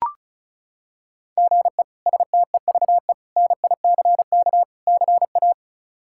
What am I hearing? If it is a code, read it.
Morse code, heard as a keyed tone, about 36 wpm, 700 Hz. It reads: GE STEVE NICK CA
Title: K1USN SST QSO Practice - 35wpm